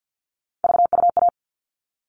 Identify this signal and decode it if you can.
Morse code, heard as a keyed tone, about 50 wpm, 700 Hz. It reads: VUA
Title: All Cut Numbers - 3-Digit 50wpm